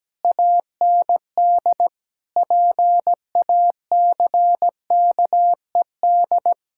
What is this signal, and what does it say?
Morse code, heard as a keyed tone, about 17 wpm, 700 Hz. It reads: AND PACKED
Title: Pangram Challenge v2 - 17wpm